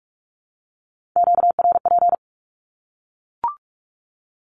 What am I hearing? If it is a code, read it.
Morse code, heard as a keyed tone, about 44 wpm, 700 Hz. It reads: QRP
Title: Q-Codes Repeated 3x 45wpm